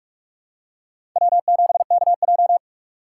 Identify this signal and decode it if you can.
Morse code, heard as a keyed tone, about 42 wpm, 700 Hz. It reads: W7KJ
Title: USA Call Signs - 6x Character Spacing - 45wpm